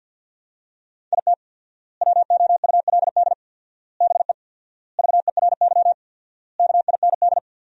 Morse code, heard as a keyed tone, about 50 wpm, 700 Hz. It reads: IT WOULD BE VERY KIND
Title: 4x Word-Spacing - Sentences from Top 200 Words - 50wpm